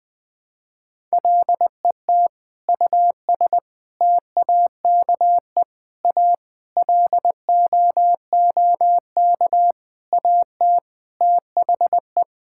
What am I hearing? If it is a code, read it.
Morse code, heard as a keyed tone, about 20 wpm, 700 Hz. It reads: LET US TAKE A LOOK AT THE
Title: Sentences with 5-Word Phrases - 20wpm